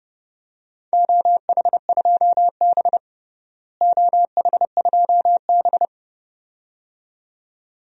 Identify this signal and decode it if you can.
Morse code, heard as a keyed tone, about 29 wpm, 700 Hz. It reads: OH2B OH2B
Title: DXCC-100 Call Signs - 4 Characters - Repeated 2x - 30wpm